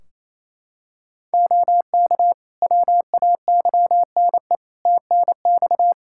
Morse code, heard as a keyed tone, about 28 wpm, 700 Hz. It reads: OK WAYNE TNX
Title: Practice QSO - Exchange 4 - Ending - 28wpm